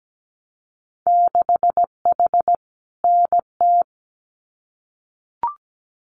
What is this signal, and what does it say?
Morse code, heard as a keyed tone, about 17 wpm, 700 Hz. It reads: 6H NT
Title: Winter Field Day Exchanges Encore 17wpm